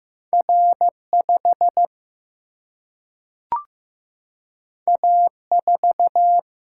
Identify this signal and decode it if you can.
Morse code, heard as a keyed tone, about 15 wpm, 700 Hz. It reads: R 5 A 4
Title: Single-Letter Single-Number 15wpm